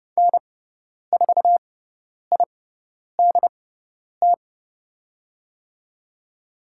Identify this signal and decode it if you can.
Morse code, heard as a keyed tone, about 30 wpm, spaced as for 10 wpm, 700 Hz. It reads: N4IDT
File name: Lesson 16g - Letter c - Call Signs 30wpm